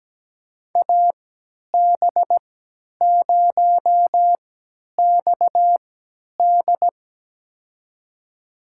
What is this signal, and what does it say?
Morse code, heard as a keyed tone, about 17 wpm, spaced as for 10 wpm, 700 Hz. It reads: AB0XD
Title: USA Call Signs - 3x Character Spacing - 17wpm